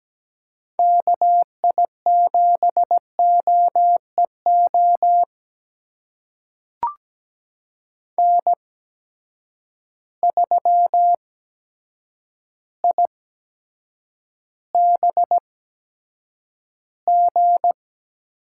Morse code, heard as a keyed tone, about 17 wpm, spaced as for 8 wpm, 700 Hz. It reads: KI7OEO N3IBG
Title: USA Call Signs - 8x Character Spacing - 17wpm